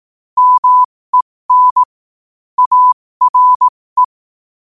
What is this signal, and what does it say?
Morse code, heard as a keyed tone, about 18 wpm, spaced as for 15 wpm, 1000 Hz. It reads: MEN ARE
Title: 15 WPM morse code quotes for Thu, 29 Feb 2024